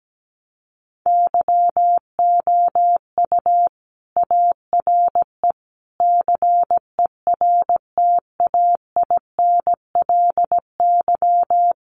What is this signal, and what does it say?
Morse code, heard as a keyed tone, about 17 wpm, 700 Hz. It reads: YOU ARE CERTAINLY
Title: Sentences from Top 1200 Words 17wpm